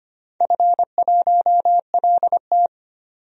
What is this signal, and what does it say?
Morse code, heard as a keyed tone, about 25 wpm, 700 Hz. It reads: F1LT
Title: DXCC-100 Call Signs - 4 Characters - 25wpm